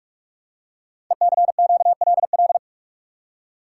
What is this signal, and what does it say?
Morse code, heard as a keyed tone, about 44 wpm, 700 Hz. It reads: ECQLL
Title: Code Groups - 5 Letters - 45wpm